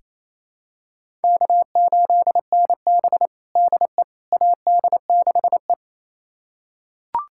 Morse code, heard as a keyed tone, about 27 wpm, 700 Hz. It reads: K8NB DE AD6E
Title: USA Callsign DE Callsign - 4 Characters - 28wpm